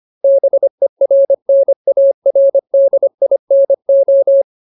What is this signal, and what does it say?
Morse code, heard as a keyed tone, about 25 wpm, 550 Hz. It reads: BERNARDINO
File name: Ultimate Word List / 25WPM Ultimate